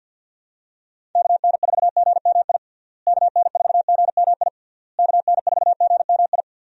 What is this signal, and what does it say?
Morse code, heard as a keyed tone, about 50 wpm, 700 Hz. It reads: KN4GMI KN4GMI KN4GMI
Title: USA Call Signs - 6 Characters - Repeated 3x - 50wpm